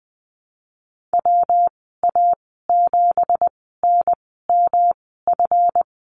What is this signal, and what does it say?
Morse code, heard as a keyed tone, about 20 wpm, spaced as for 15 wpm, 700 Hz. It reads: WA7NMF
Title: 179 Lesson 29g - Slash - Call Signs 20wpm